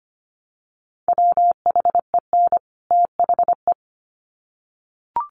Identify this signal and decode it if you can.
Morse code, heard as a keyed tone, about 25 wpm, 700 Hz. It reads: WHEN THE
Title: Sets of 2 Words 25wpm